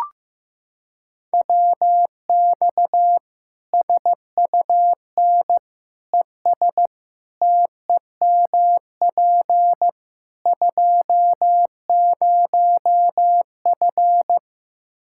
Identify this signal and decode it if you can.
Morse code, heard as a keyed tone, about 15 wpm, 700 Hz. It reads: WX SUN ES TEMP 20F
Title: Practice QSO - Exchange 2 - Info - 15wpm